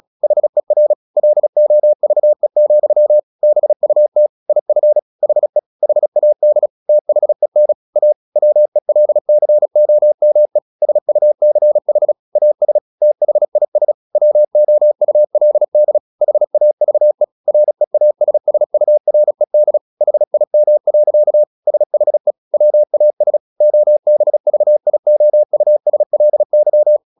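Morse code, heard as a keyed tone, about 37 wpm, 600 Hz. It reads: HER LOVE, BUT IF HE HAD THEN A WELCOME SUCH AS THIS WOULD HAVE REASSURED HIM. SHE WAS OBVIOUSLY
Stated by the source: Morse practice at 36 wpm, text of Ken Follett's novel Never, from CW med Gnister.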